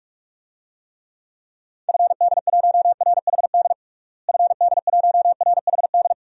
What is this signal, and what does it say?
Morse code, heard as a keyed tone, about 47 wpm, 700 Hz. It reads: FD1RHD FD1RHD
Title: DXCC-100 Call Signs - 6 Characters - Repeated 2x - 45wpm